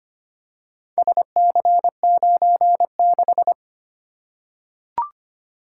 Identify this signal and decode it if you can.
Morse code, heard as a keyed tone, about 25 wpm, 700 Hz. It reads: SC96
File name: practice / 25wpm / Four Letter-Number Combinations - 25wpm